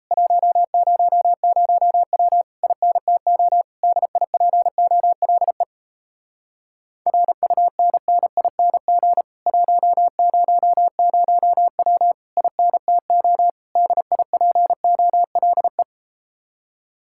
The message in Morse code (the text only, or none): 100W INTO DIPOLE RUNNING 100W INTO DIPOLE